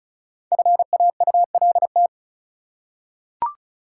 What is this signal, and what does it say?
Morse code, heard as a keyed tone, about 36 wpm, 700 Hz. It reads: FAULT
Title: Five Letter Words - 35wpm